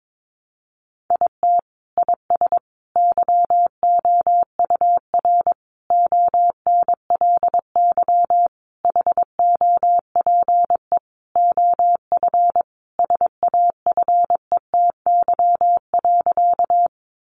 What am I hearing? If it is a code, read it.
Morse code, heard as a keyed tone, about 22 wpm, 700 Hz. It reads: IT IS YOUR ONLY HOPE OF SAFETY.
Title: Sentences from Top 1000 Words 22wpm